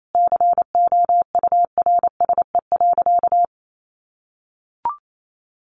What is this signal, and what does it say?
Morse code, heard as a keyed tone, about 28 wpm, 700 Hz. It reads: COURSE.
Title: Sentences from Top 1200 Words 28wpm